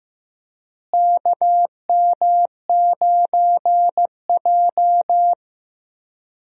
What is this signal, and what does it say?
Morse code, heard as a keyed tone, about 15 wpm, 700 Hz. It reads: KM9J
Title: USA Call Signs - 4 Characters - Repeated 3x - 15wpm